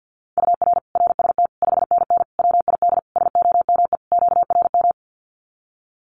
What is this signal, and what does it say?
Morse code, heard as a keyed tone, about 50 wpm, 700 Hz. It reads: UR RST 5NN WID SOME QRM
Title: Common QSO Phrases 50wpm